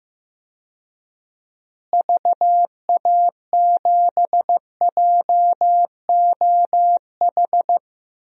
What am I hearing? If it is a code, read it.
Morse code, heard as a keyed tone, about 16 wpm, 700 Hz. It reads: VA7JOH
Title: DXCC-100 Call Signs - 6 Characters - Repeated 2x - 15wpm